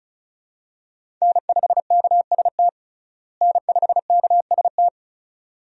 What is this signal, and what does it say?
Morse code, heard as a keyed tone, about 34 wpm, 700 Hz. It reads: N5KST N5KST
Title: USA Call Signs - 5 Characters - Repeated 2x - 35wpm